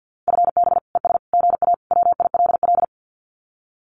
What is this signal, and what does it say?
Morse code, heard as a keyed tone, about 48 wpm, 700 Hz. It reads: FB ES GA WILL
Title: Common QSO Phrases Encore 50wpm